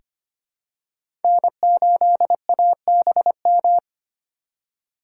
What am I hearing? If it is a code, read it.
Morse code, heard as a keyed tone, about 25 wpm, 700 Hz. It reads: N8ABM
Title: USA Call Signs - 5 Characters - 4x Character Spacing - 25wpm